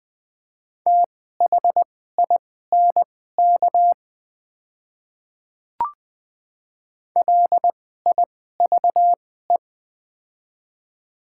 Morse code, heard as a keyed tone, about 20 wpm, spaced as for 15 wpm, 700 Hz. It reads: THINK LIVE
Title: 2x Character Spacing - Top 500 Words - 20wpm